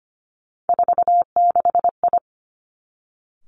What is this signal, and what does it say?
Morse code, heard as a keyed tone, about 25 wpm, 700 Hz. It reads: 46I
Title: Three Challenging Letter-Number Combinations 25wpm